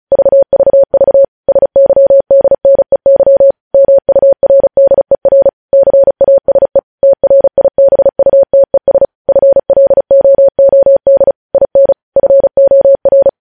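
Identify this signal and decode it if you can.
Morse code, heard as a keyed tone, about 36 wpm, 550 Hz. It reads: VVV SYDNEY MURDER CASE TRIBUTES FLOOD IN FOR
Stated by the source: Morse news headlines practice at 35 words per minute.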